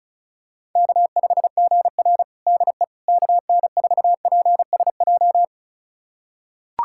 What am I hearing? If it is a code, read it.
Morse code, heard as a keyed tone, about 34 wpm, 700 Hz. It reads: K5GR DE KN4PSJ
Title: USA Callsign DE Callsign 35wpm